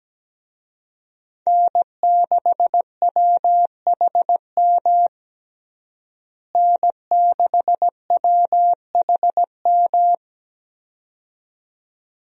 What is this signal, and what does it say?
Morse code, heard as a keyed tone, about 17 wpm, 700 Hz. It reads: N6WHM N6WHM
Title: USA Call Signs - 5 Characters - Repeated 2x - 17wpm